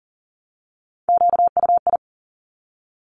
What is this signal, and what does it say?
Morse code, heard as a keyed tone, about 42 wpm, 700 Hz. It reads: QUI
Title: Top Three Letter Combinations in Words 40wpm